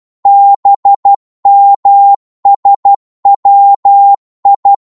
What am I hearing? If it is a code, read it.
Morse code, heard as a keyed tone, about 12 wpm, 800 Hz. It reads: BMSWI
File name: Morse / ltr-12wpm